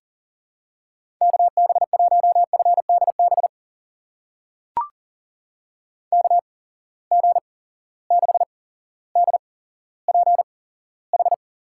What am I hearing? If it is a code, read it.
Morse code, heard as a keyed tone, about 38 wpm, spaced as for 17 wpm, 700 Hz. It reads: KB1FDB KG6DPH
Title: USA Call Signs - 6 Characters - 8x Character Spacing - 40wpm